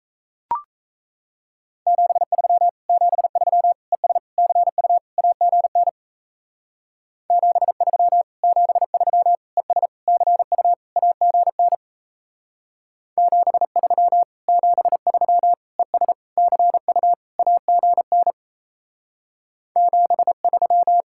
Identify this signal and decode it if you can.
Morse code, heard as a keyed tone, about 36 wpm, 700 Hz. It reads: 73 73 ES CU AGN 73 73 ES CU AGN 73 73 ES CU AGN 73